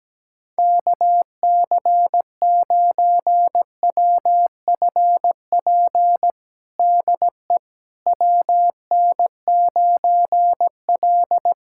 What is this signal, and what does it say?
Morse code, heard as a keyed tone, about 17 wpm, 700 Hz. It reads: KC9WFP DE WN9L
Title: USA Callsign DE Callsign 17wpm